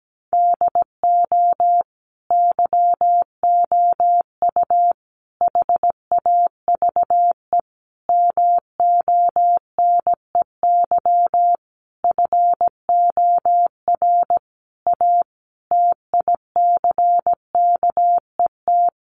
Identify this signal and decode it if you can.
Morse code, heard as a keyed tone, about 17 wpm, 700 Hz. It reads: DO YOU HAVE MONEY FOR A TICKET
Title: Sentences from Top 1400 Words 17wpm